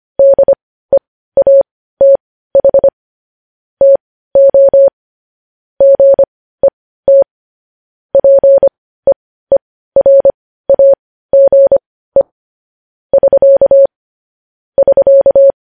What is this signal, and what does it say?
Morse code, heard as a keyed tone, about 25 wpm, spaced as for 15 wpm, 550 Hz. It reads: DEATH TO GET PEERAGE <SK> <SK>
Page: News Headlines in Morse Code at 15 WPM